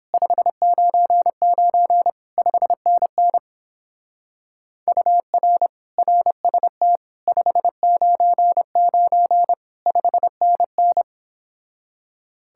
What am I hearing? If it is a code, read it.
Morse code, heard as a keyed tone, about 29 wpm, 700 Hz. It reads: 599 5NN UR RST 599 5NN